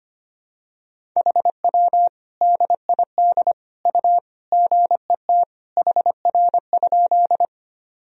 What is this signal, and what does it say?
Morse code, heard as a keyed tone, about 25 wpm, 700 Hz. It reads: HW DID U GET HR?
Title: Sentences from Top 200 Words and Abbreviations 25wpm